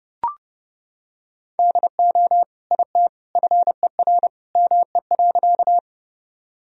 Morse code, heard as a keyed tone, about 30 wpm, 700 Hz. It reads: DO IT FER ME.
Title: Sentences from Top 200 Words and Abbreviations 30wpm